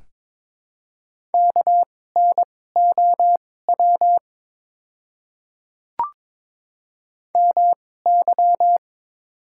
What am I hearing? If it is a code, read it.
Morse code, heard as a keyed tone, about 22 wpm, spaced as for 16 wpm, 700 Hz. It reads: KNOW MY
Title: 2x Character Spacing - Top 100 Words - 22wpm